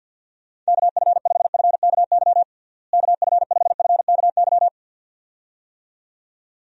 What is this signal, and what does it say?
Morse code, heard as a keyed tone, about 51 wpm, 700 Hz. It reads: KF5FKY KF5FKY
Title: USA Call Signs - 6 Characters - Repeated 2x - 50wpm